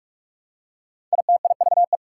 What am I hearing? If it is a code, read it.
Morse code, heard as a keyed tone, about 44 wpm, 690 Hz. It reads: ITIVE
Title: Prefixes and Suffixes 45wpm